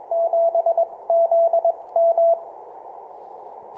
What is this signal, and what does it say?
Morse code, heard as a keyed tone, about 21 wpm, spaced as for 18 wpm, 670 Hz. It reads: 7ZM
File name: simple-file-list / 4L DL7ZM 6M CW